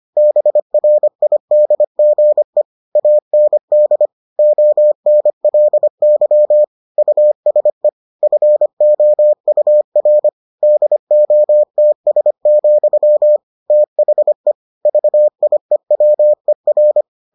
Morse code, heard as a keyed tone, about 25 wpm, 600 Hz. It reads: BRIDGE AND ONLY USE FOUR DOTS, THE VIEWER